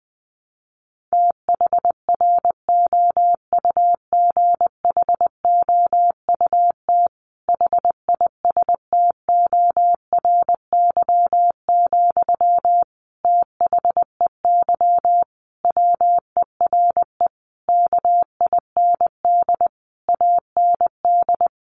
Morse code, heard as a keyed tone, about 20 wpm, 700 Hz. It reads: THROUGHOUT HISTORY, THEY WERE KIND AND
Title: Sentences from Top 800 Words 20wpm